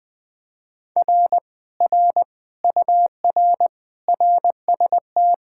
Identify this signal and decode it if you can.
Morse code, heard as a keyed tone, about 20 wpm, 700 Hz. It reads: R R UR RST
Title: Speed-Racing Common QSO Phrases - 17wpm